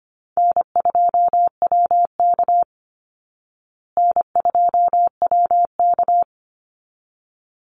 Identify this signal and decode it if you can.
Morse code, heard as a keyed tone, about 25 wpm, 700 Hz. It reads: N2WK N2WK
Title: Call Signs Repeated 3x - 25wpm